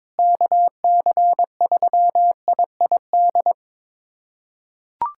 Morse code, heard as a keyed tone, about 22 wpm, 700 Hz. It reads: KC3IID
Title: USA Callsign DE Callsign - 6 Characters - 22wpm